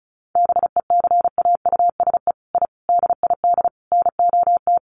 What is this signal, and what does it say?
Morse code, heard as a keyed tone, about 34 wpm, 700 Hz. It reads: BECAUSE I DID NOT
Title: Sets of 5 Words Encore2 35wpm